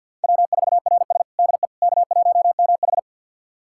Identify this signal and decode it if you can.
Morse code, heard as a keyed tone, about 48 wpm, 700 Hz. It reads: W4RS DE K1MH
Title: USA Callsign DE Callsign - 4 Characters - 50wpm